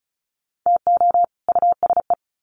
Morse code, heard as a keyed tone, about 34 wpm, 700 Hz. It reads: TO USE
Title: Sets of 2 Words 35wpm